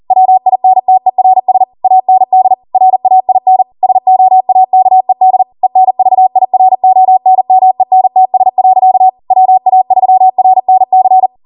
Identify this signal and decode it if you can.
Morse code, heard as a keyed tone, about 40 wpm, 750 Hz. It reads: WINTERS AND RAIN SOAKED ENVIRONMENTS. WA3RNC
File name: Morse / 40WPM